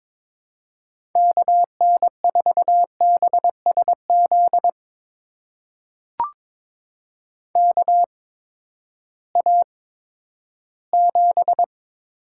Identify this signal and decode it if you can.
Morse code, heard as a keyed tone, about 22 wpm, spaced as for 13 wpm, 700 Hz. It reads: KN4BSZ KA7
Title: USA Call Signs - 6 Characters - 8x Character Spacing - 22wpm